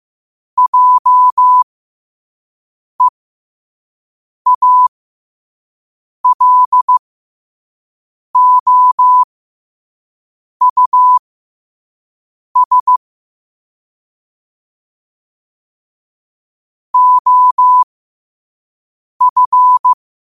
Morse code, heard as a keyed tone, about 15 wpm, spaced as for 5 wpm, 1000 Hz. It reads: JEALOUS OF
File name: qotdfiles / 5wpm